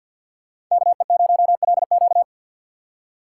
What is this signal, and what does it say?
Morse code, heard as a keyed tone, about 48 wpm, 700 Hz. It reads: KE0LQ
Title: USA Call Signs - 5 Characters - 2x Character Spacing - 50wpm